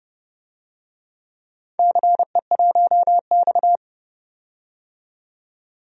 Morse code, heard as a keyed tone, about 30 wpm, 700 Hz. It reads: CE1X
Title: DXCC-100 Call Signs - 4 Characters - 30wpm